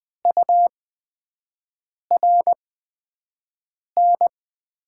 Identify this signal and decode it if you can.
Morse code, heard as a keyed tone, about 20 wpm, spaced as for 5 wpm, 700 Hz. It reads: URN
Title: 8x Character Spacing - Top 200 Words - 20wpm